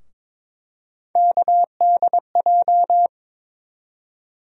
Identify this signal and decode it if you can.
Morse code, heard as a keyed tone, about 22 wpm, 700 Hz. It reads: KDJ
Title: USA Call Sign Postfixes - 3 Letters 22wpm